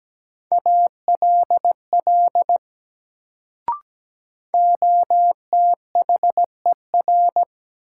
Morse code, heard as a keyed tone, about 17 wpm, 700 Hz. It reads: ALL OTHER
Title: Sentences from Top 100 Words - New Word Drill - 17wpm